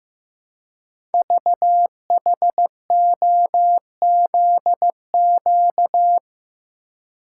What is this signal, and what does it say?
Morse code, heard as a keyed tone, about 15 wpm, 700 Hz. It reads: VHOZQ
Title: Code Groups - 5 Letters - 15wpm